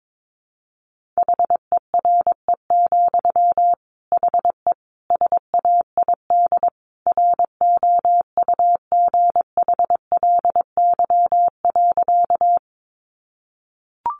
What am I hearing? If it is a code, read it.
Morse code, heard as a keyed tone, about 22 wpm, 700 Hz. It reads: HERE, HE SAID ROUGHLY.
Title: Sentences from Top 1500 Words 22wpm